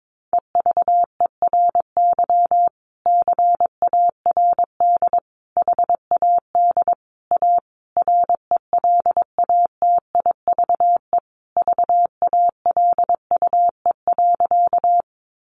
Morse code, heard as a keyed tone, about 22 wpm, 700 Hz. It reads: EVERY CARD HAD A RELATIVE VALUE.
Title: Sentences from Top 1400 Words 22wpm